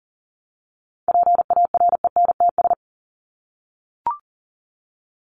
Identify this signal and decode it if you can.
Morse code, heard as a keyed tone, about 40 wpm, 700 Hz. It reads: PARENTS
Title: ES:S Suffix - Top 100 Words with Base Words 40wpm